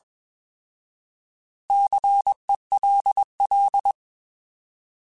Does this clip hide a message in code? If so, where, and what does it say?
Yes: Morse code from 1.7 to 3.9 s, heard as a keyed tone, about 21 wpm, 800 Hz. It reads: CELL